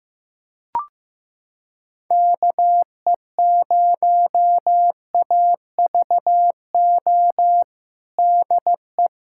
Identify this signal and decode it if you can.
Morse code, heard as a keyed tone, about 15 wpm, 700 Hz. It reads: KE0AVO DE
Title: USA Callsign DE Callsign 15wpm